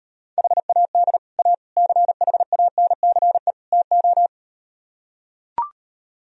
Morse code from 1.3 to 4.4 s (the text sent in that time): A CHANCE TO